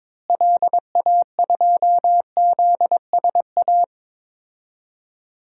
Morse code, heard as a keyed tone, about 22 wpm, 700 Hz. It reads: LA2ZSA
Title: DXCC-100 Call Signs - 6 Characters - Repeated 2x - 22wpm